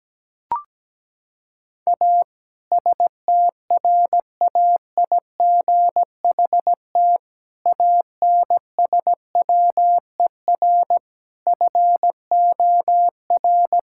Morse code, heard as a keyed tone, about 17 wpm, 700 Hz. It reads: A STRAIGHT ANSWER FOR
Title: Sentences from Top 1000 Words - Review Part 6 - 17wpm